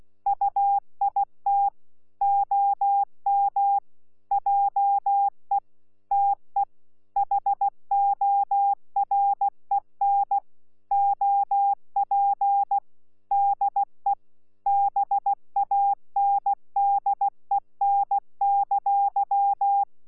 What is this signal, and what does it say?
Morse code, heard as a keyed tone, about 16 wpm, 800 Hz. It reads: UIT OM JE TE HOREN OP DE BANDEN!
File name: Z PI4VRZ / PI4VRZ 14-jun-2025